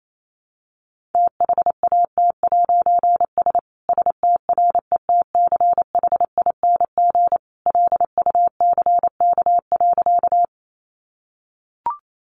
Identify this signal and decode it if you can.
Morse code, heard as a keyed tone, about 28 wpm, 700 Hz. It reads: THAT'S STRETCHING LUCK.
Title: A Sound Of Thunder 28wpm